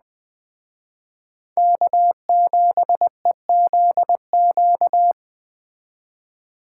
Morse code, heard as a keyed tone, about 20 wpm, 700 Hz. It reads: K7EZQ